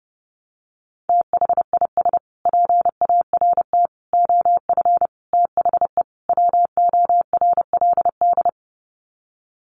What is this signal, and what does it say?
Morse code, heard as a keyed tone, about 30 wpm, 700 Hz. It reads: THIS PART OF THE WORLD
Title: Sets of 5 Words Encore2 30wpm